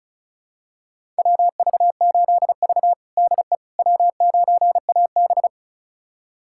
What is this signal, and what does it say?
Morse code, heard as a keyed tone, about 36 wpm, 700 Hz. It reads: WV8V DE W9AB
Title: USA Callsign DE Callsign - 4 Characters - 35wpm